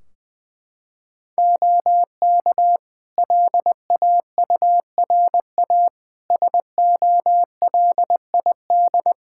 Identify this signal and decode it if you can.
Morse code, heard as a keyed tone, about 21 wpm, 700 Hz. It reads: OK LAURA SOLID
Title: K1USN SST QSO Practice 20wpm